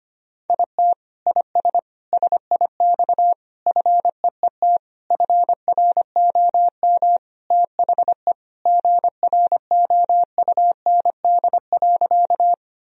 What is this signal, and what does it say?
Morse code, heard as a keyed tone, about 25 wpm, 700 Hz. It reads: IT IS SIX FEET FROM THE GROUND.
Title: Sentences from Top 1000 Words - Review Part 2 - 25wpm